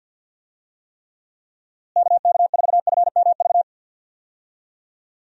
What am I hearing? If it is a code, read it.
Morse code, heard as a keyed tone, about 51 wpm, 700 Hz. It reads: KK4FMV
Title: USA Call Signs - 6 Characters - Repeated 2x - 50wpm